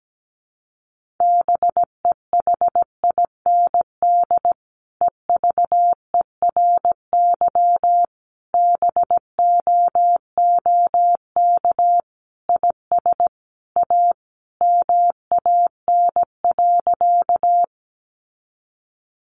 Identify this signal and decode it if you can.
Morse code, heard as a keyed tone, about 17 wpm, 700 Hz. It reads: BEHIND EVERY BOOK IS A MAN.
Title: Sentences from Top 500 Words 17wpm